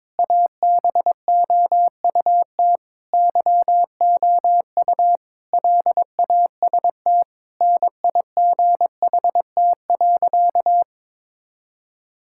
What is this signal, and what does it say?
Morse code, heard as a keyed tone, about 22 wpm, 700 Hz. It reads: ABOUT YOU LAST NIGHT.
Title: Sentences from Top 1600 Words 22wpm